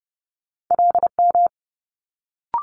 Morse code, heard as a keyed tone, about 30 wpm, 700 Hz. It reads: LM
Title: Multi-letter Phonemes Quick Fire 30wpm